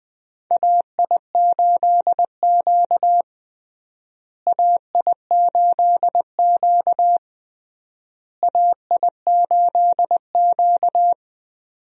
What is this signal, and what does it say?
Morse code, heard as a keyed tone, about 20 wpm, 700 Hz. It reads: AI8Q AI8Q AI8Q
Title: USA Call Signs - 4 Characters - Repeated 3x - 20wpm